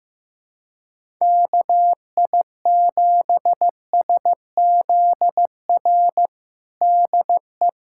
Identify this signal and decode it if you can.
Morse code, heard as a keyed tone, about 15 wpm, 700 Hz. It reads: KI7SZR DE
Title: USA Callsign DE Callsign - 6 Characters - 15wpm